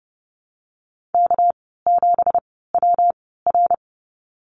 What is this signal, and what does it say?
Morse code, heard as a keyed tone, about 30 wpm, spaced as for 17 wpm, 700 Hz. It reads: K7WR
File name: Lesson 37g - Letter x - Call Signs 30wpm